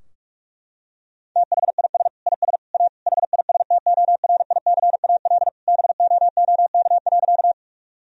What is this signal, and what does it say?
Morse code, heard as a keyed tone, about 46 wpm, 700 Hz. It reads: THIS IS A HISTORICAL BOOK.
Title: Sentences from Top 1000 Words - Review Part 4 - 45wpm